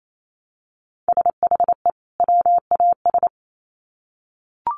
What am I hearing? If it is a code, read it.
Morse code, heard as a keyed tone, about 28 wpm, 700 Hz. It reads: SHE WAS
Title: Sets of 2 Words 28wpm